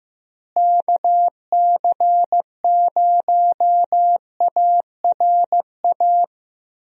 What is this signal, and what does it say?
Morse code, heard as a keyed tone, about 15 wpm, 700 Hz. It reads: KC0ARA
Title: USA Call Signs - 6 Characters - Repeated 2x - 15wpm